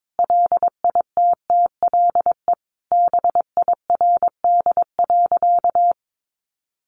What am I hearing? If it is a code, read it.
Morse code, heard as a keyed tone, about 22 wpm, 700 Hz. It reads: LITTLE BIRD.
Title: Sentences from Top 1100 Words 22wpm